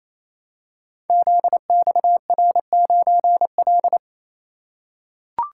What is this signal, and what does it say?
Morse code, heard as a keyed tone, about 28 wpm, 700 Hz. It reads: ZXR9L
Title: Code Groups - 5 Letter:Numbers - 28wpm